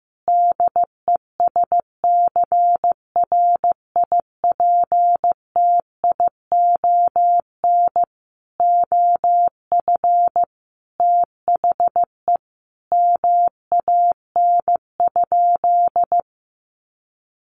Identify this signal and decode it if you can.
Morse code, heard as a keyed tone, about 15 wpm, 700 Hz. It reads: DESCRIPTION OF THE MAN?
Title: Sentences from Top 1500 Words 15wpm